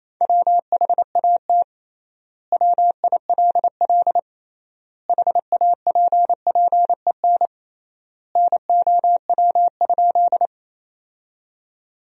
Morse code, heard as a keyed tone, about 28 wpm, 700 Hz. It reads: WHAT WILL HAPPEN NOW?
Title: 3x Word-Spacing - Sentences from Top 400 Words - 28wpm